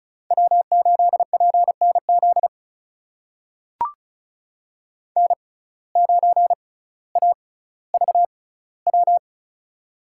Morse code, heard as a keyed tone, about 36 wpm, spaced as for 18 wpm, 700 Hz. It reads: W8PNZ N9AVW
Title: USA Call Signs - 5 Characters - 6x Character Spacing - 35wpm